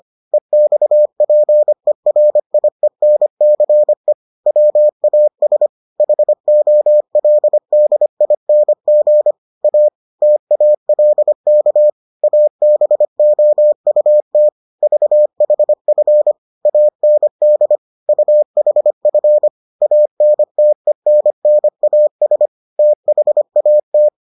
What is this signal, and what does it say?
Morse code, heard as a keyed tone, about 25 wpm, 600 Hz. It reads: EXPERIENCE WAS HOLDING A TALK ABOUT VHF AND UHF ANTENNAS THAT